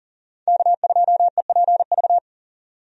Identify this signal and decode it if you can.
Morse code, heard as a keyed tone, about 41 wpm, 700 Hz. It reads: K2EPV
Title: USA Call Signs - 5 Characters - 40wpm